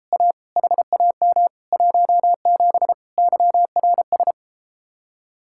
Morse code, heard as a keyed tone, about 33 wpm, 700 Hz. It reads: A HAM 17 YRS